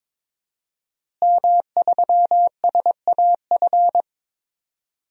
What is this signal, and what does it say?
Morse code, heard as a keyed tone, about 21 wpm, 700 Hz. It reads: M3SAF
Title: DXCC-100 Call Signs - 5 Characters - Repeated 3x - 22wpm